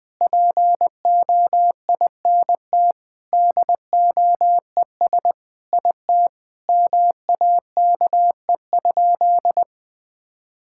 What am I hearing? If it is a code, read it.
Morse code, heard as a keyed tone, about 20 wpm, 700 Hz. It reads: POINT DOES IT MAKE?
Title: Sentences with 8-Word Phrases - 20wpm